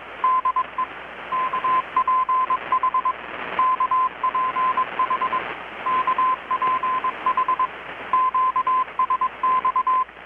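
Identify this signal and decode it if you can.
Morse code, heard as a keyed tone, about 21 wpm, 1030 Hz, with radio noise. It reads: DE KPH KPH KPH QSX
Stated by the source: Morse from KPH's Night of Nights XX maritime radiotelegraphy event, received in Chiapas, Mexico about 3,800 km away on a Kenwood R-600 with a grounded 10 m long-wire antenna.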